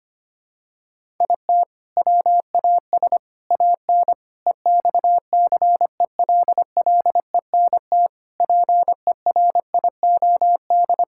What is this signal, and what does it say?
Morse code, heard as a keyed tone, about 25 wpm, 700 Hz. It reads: IT WAS AN EXCELLENT PERIOD
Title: Sentences from Top 1000 Words - Review Part 5 - 25wpm